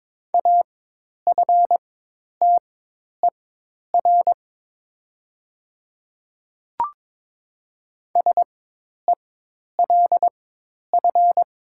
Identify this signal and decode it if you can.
Morse code, heard as a keyed tone, about 22 wpm, spaced as for 10 wpm, 700 Hz. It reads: AFTER SELF
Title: 4x Character Spacing - Top 500 Words - 22wpm